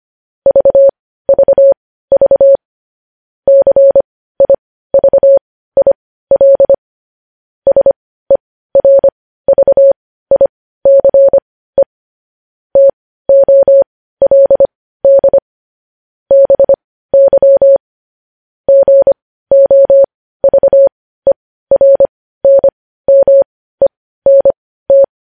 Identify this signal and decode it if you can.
Morse code, heard as a keyed tone, about 25 wpm, spaced as for 15 wpm, 550 Hz. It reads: VVV CIVIL SERVICE TOLD BY GOVERNMENT
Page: news-headlines-in-morse-code-at-15-wpm